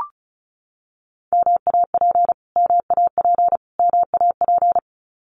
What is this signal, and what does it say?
Morse code, heard as a keyed tone, about 34 wpm, 700 Hz. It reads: MAP MAP MAP
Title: Top 500 Words Repeated 35wpm